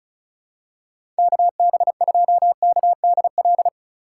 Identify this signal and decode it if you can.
Morse code, heard as a keyed tone, about 36 wpm, 700 Hz. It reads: KB2KDL
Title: USA Call Signs - 6 Characters - 2x Character Spacing - 35wpm